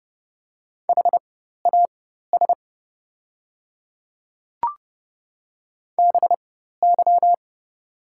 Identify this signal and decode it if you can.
Morse code, heard as a keyed tone, about 30 wpm, spaced as for 14 wpm, 690 Hz. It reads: HAS BY
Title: 4x Character Spacing - Top 100 Words - 30wpm